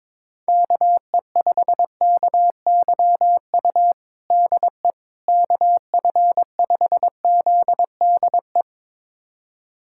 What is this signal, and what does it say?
Morse code, heard as a keyed tone, about 22 wpm, 700 Hz. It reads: KE5KYU DE KF5ZDE
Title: USA Callsign DE Callsign 22wpm